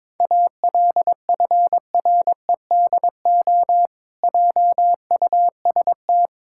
Morse code, heard as a keyed tone, about 22 wpm, 700 Hz. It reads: ALFREDO JUST
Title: Pangram Challenge v2 - 22wpm